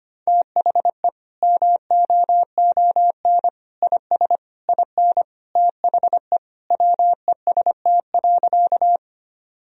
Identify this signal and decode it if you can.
Morse code, heard as a keyed tone, about 25 wpm, 700 Hz. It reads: THE MOON IS IN THE WEST.
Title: Sentences from Top 1000 Words - Review Part 3 - 25wpm